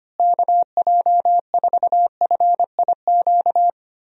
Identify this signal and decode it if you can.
Morse code, heard as a keyed tone, about 25 wpm, 700 Hz. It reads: KJ4FIQ
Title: USA Call Signs - 6 Characters - Repeated 3x - 25wpm